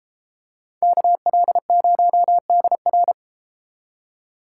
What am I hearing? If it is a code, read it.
Morse code, heard as a keyed tone, about 31 wpm, 700 Hz. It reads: KL0DR